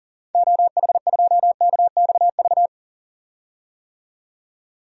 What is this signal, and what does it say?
Morse code, heard as a keyed tone, about 40 wpm, 700 Hz. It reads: OH2KXV
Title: DXCC-100 Call Signs - 6 Characters - 40wpm